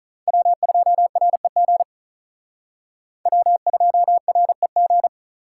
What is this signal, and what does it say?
Morse code, heard as a keyed tone, about 39 wpm, 700 Hz. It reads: W2REG W2REG
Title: Speed Racing - USA 1x3 Call Signs 35wpm